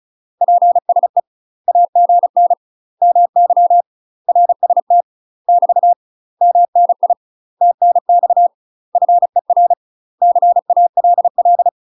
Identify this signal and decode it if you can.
Morse code, heard as a keyed tone, about 34 wpm, 700 Hz. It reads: PSE AGN MY RST <BT> MNI TNX FER CALL
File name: UK MASTER QSOs TRAINING / 35WPM / UK MASTER QSOs 0015